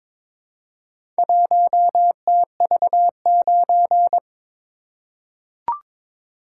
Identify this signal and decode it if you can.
Morse code, heard as a keyed tone, about 22 wpm, 700 Hz. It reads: 1TV9
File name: practice / 22wpm / Four Letter-Number Combinations - 22wpm